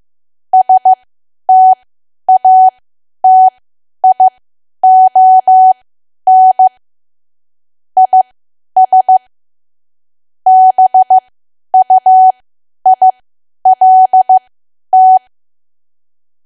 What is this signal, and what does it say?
Morse code, heard as a keyed tone, about 15 wpm, spaced as for 10 wpm, 750 Hz. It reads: STATION IS BUILT
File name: Archive / 10 WPM / 10WPM